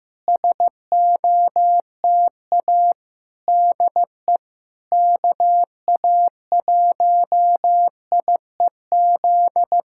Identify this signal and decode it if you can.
Morse code, heard as a keyed tone, about 15 wpm, 700 Hz. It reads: SOTA DE KA1IEZ
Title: SOTA QSO Practice 15wpm